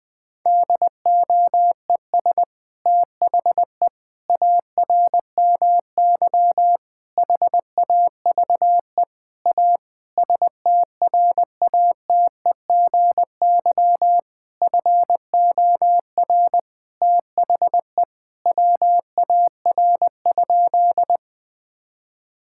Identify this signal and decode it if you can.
Morse code, heard as a keyed tone, about 20 wpm, 700 Hz. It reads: DOES THE ARMY HAVE A STRATEGY FOR THE WAR?
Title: Sentences from Top 1000 Words - Review Part 5 - 20wpm